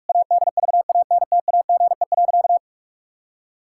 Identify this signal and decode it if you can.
Morse code, heard as a keyed tone, about 46 wpm, 700 Hz. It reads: ADVANTAGE.
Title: Sentences from Top 1000 Words - Review Part 6 - 45wpm